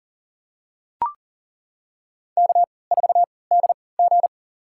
Morse code, heard as a keyed tone, about 40 wpm, spaced as for 23 wpm, 700 Hz. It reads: K4DG
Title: USA Call Signs - 3x Character Spacing - 40wpm